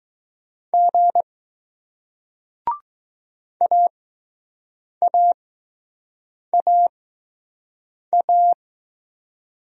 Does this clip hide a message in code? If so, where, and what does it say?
Yes: Morse code from 0.7 to 8.5 s, heard as a keyed tone, about 20 wpm, spaced as for 6 wpm, 700 Hz. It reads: G AAAA